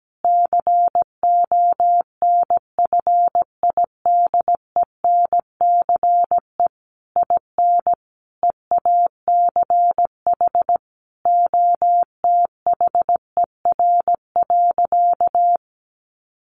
Morse code, heard as a keyed tone, about 17 wpm, 700 Hz. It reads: CONFIDENCE IN EACH OTHER.
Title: Sentences from Top 1500 Words 17wpm